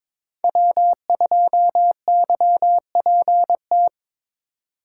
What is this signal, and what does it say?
Morse code, heard as a keyed tone, about 22 wpm, 700 Hz. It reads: W2YPT
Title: USA Call Signs - 5 Characters - 3x Character Spacing - 22wpm